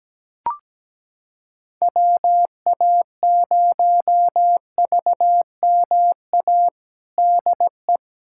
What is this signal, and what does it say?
Morse code, heard as a keyed tone, about 17 wpm, 700 Hz. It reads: WA0VMA DE
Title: USA Callsign DE Callsign 17wpm